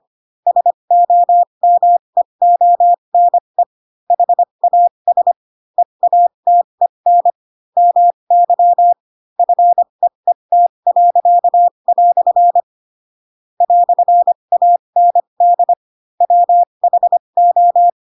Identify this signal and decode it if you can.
Morse code, heard as a keyed tone, about 25 wpm, spaced as for 22 wpm, 700 Hz. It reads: SOMEONE HAS EATEN MY FEET." "AND WHO
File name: Pinocchio / Pinocchio ch7 18 25 ik1hge